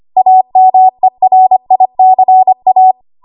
Morse code, heard as a keyed tone, about 25 wpm, 750 Hz. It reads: AMERICA
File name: Morse / 25WPM